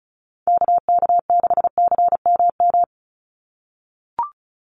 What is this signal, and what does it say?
Morse code, heard as a keyed tone, about 34 wpm, 700 Hz. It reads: KK6CMM
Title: USA Call Signs 35wpm